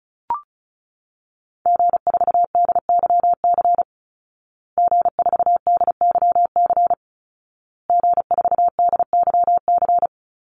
Morse code, heard as a keyed tone, about 34 wpm, 700 Hz. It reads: G4DYC G4DYC G4DYC
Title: Call Signs Repeated 2x - 35wpm